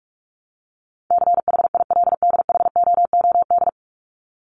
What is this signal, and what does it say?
Morse code, heard as a keyed tone, about 44 wpm, 700 Hz. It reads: CHILDHOOD
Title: Top 1751-2000 Words 45wpm